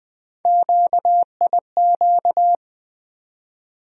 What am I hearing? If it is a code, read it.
Morse code, heard as a keyed tone, about 20 wpm, 700 Hz. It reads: QIQ
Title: USA Call Sign Postfixes - 3 Letters 20wpm